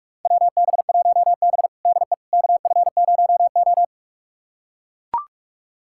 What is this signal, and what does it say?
Morse code, heard as a keyed tone, about 41 wpm, 700 Hz. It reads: WB1B DE KF0O
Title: USA Callsign DE Callsign - 4 Characters - 45wpm